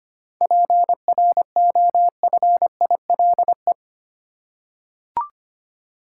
Morse code, heard as a keyed tone, about 25 wpm, 700 Hz. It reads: PROFILE
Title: Seven Letter Words - 25wpm